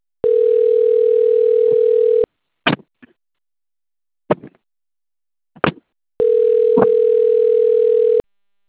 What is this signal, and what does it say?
Ringback tone (North America), 440 + 480 Hz.
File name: out- - - - - .24545